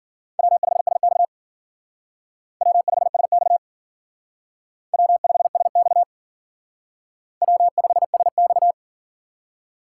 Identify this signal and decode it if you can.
Morse code, heard as a keyed tone, about 51 wpm, 700 Hz. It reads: W5SX W5SX W5SX W5SX